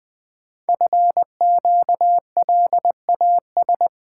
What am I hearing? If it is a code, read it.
Morse code, heard as a keyed tone, about 20 wpm, 700 Hz. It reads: FQLAS
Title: Code Groups - 5 Letters - 20wpm